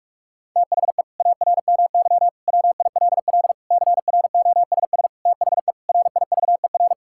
Morse code, heard as a keyed tone, about 41 wpm, 700 Hz. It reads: THE ARMY WILL CROSS THE RIVER
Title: Sentences from Top 1000 Words - Review Part 5 - 45wpm